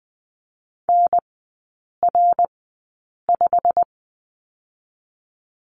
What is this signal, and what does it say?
Morse code, heard as a keyed tone, about 20 wpm, 700 Hz. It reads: N R 5
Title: 114 Lesson 19e - Letter w - Three Characters 20wpm